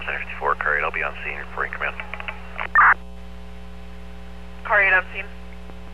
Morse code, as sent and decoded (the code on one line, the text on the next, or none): none